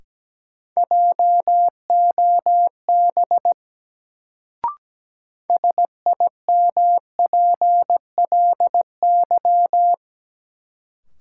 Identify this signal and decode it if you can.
Morse code, heard as a keyed tone, about 17 wpm, 700 Hz. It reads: JOB SIMPLY
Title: Sentences from Top 600 Words - New Word Drill - 17wpm